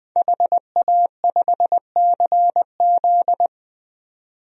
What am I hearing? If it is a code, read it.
Morse code, heard as a keyed tone, about 20 wpm, 700 Hz. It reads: HA5CZ
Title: DXCC-100 Call Signs - 5 Characters - 20wpm